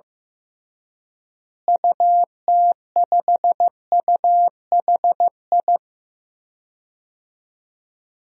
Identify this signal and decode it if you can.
Morse code, heard as a keyed tone, about 16 wpm, 700 Hz. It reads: UT5UHI
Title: DXCC-100 Call Signs - 6 Characters - Repeated 2x - 15wpm